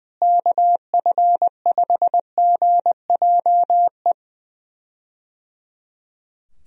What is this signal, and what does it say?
Morse code, heard as a keyed tone, about 21 wpm, 700 Hz. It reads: KF5GJE
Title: USA Call Signs - 6 Characters - Repeated 3x - 20wpm